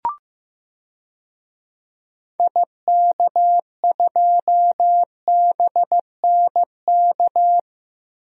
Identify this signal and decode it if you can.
Morse code, heard as a keyed tone, about 15 wpm, 700 Hz. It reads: IK2BNK
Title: DXCC-100 Call Signs - 6 Characters - Repeated 2x - 15wpm